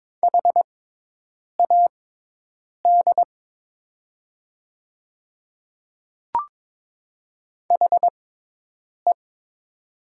Morse code, heard as a keyed tone, about 22 wpm, spaced as for 8 wpm, 700 Hz. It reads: HAD HE
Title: 6x Character Spacing - Top 100 Words - 22wpm